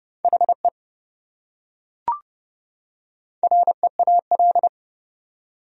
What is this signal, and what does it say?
Morse code, heard as a keyed tone, about 27 wpm, 700 Hz. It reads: HE REAL